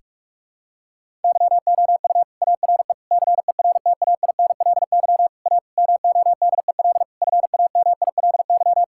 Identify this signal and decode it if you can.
Morse code, heard as a keyed tone, about 46 wpm, 700 Hz. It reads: YOU ARE CERTAINLY A MODEL FAMILY
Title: Sentences from Top 1000 Words - Review Part 5 - 45wpm